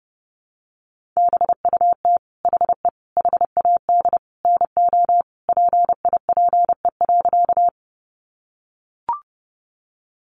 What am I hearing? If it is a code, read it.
Morse code, heard as a keyed tone, about 30 wpm, 700 Hz. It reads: BUT HE HAD NO PIPE.
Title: Sentences from Top 1500 Words 30wpm